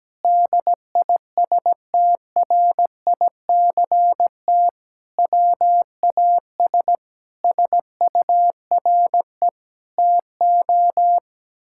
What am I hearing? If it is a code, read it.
Morse code, heard as a keyed tone, about 17 wpm, 700 Hz. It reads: DISTRICT WAS SURE TO
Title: Sentences from Top 1600 Words 17wpm